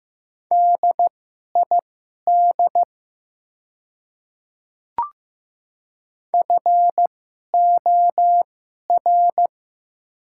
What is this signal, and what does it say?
Morse code, heard as a keyed tone, about 15 wpm, spaced as for 11 wpm, 700 Hz. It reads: DID FOR
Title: 2x Character Spacing - Top 100 Words - 15wpm